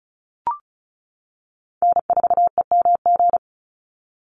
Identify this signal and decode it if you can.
Morse code, heard as a keyed tone, about 34 wpm, 700 Hz. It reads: N4EMG
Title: NA SOTA Chaser Callsigns 35wpm